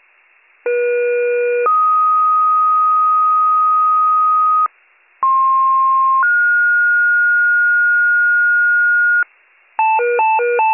Two-tone page, heard as a one-second tone, then a three-second tone, about 480 then 1250 Hz, about 1060 then 1430 Hz, with noise. It is heard from 0.7 to 9.2 s.